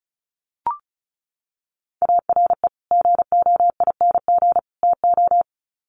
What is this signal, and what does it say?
Morse code, heard as a keyed tone, about 34 wpm, 700 Hz. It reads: ARE GOING TO
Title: Sets of 3 Words 35wpm